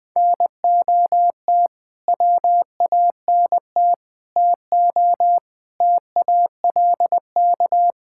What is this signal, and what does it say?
Morse code, heard as a keyed tone, about 21 wpm, 700 Hz. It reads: NOT WANT TO TALK
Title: Build A Phrase - 5-Word Phrases - 20wpm